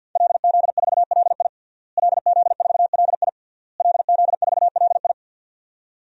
Morse code, heard as a keyed tone, about 51 wpm, 700 Hz. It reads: LZ4LI LZ4LI LZ4LI
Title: DXCC-100 Call Signs - 5 Characters - Repeated 3x - 50wpm